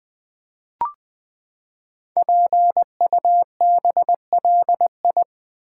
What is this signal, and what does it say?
Morse code, heard as a keyed tone, about 20 wpm, 700 Hz. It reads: PUBLI
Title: Build A Word - 5 Letters At A Time - 10-Letter Words - 20wpm